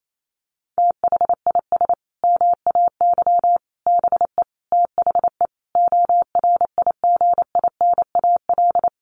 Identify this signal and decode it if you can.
Morse code, heard as a keyed tone, about 28 wpm, 700 Hz. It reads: THIS MAY BE THE ORIGINAL
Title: Sentences from Top 900 Words 28wpm